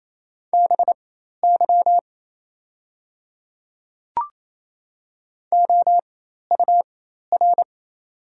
Morse code, heard as a keyed tone, about 28 wpm, spaced as for 13 wpm, 700 Hz. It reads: BY OUR
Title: 4x Character Spacing - Top 200 Words - 28wpm